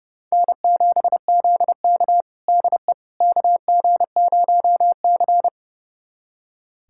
Morse code, heard as a keyed tone, about 31 wpm, 700 Hz. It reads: N7ZK DE KG0C
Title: USA Callsign DE Callsign - 4 Characters - 30wpm